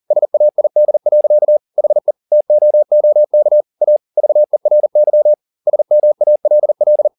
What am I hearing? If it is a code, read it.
Morse code, heard as a keyed tone, about 40 wpm, 600 Hz. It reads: SAID. HE TOOK A VERY SMALL
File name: the old man and the sea 2 40wpm